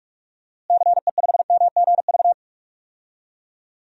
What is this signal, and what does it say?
Morse code, heard as a keyed tone, about 44 wpm, 700 Hz. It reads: CE5MGV
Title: DXCC-100 Call Signs - 6 Characters - Repeated 3x - 45wpm